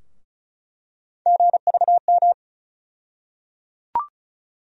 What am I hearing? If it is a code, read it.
Morse code, heard as a keyed tone, about 36 wpm, 700 Hz. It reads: GVM
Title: USA Call Sign Postfixes - 3 Letters 35wpm